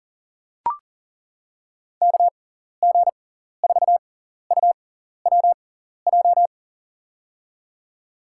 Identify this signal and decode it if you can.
Morse code, heard as a keyed tone, about 40 wpm, spaced as for 14 wpm, 700 Hz. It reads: KG4UWJ
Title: USA Call Signs - 6 Characters - 6x Character Spacing - 40wpm